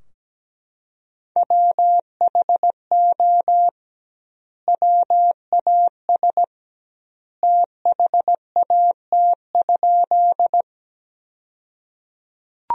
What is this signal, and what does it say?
Morse code, heard as a keyed tone, about 17 wpm, 700 Hz. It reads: WHO WAS THAT?
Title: 2x Word-Spacing - Sentences from Top 100 Words - 17wpm